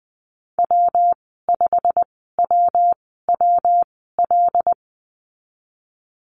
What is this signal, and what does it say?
Morse code, heard as a keyed tone, about 20 wpm, spaced as for 15 wpm, 700 Hz. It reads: W5WWL
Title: 154 Lesson 25g - Letter p - Call Signs 20wpm